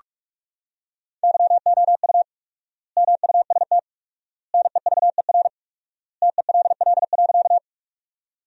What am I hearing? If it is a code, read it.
Morse code, heard as a keyed tone, about 46 wpm, 700 Hz. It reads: YOU MUST NEVER TELL.
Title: 4x Word-Spacing - Sentences from Top 300 Words - 45wpm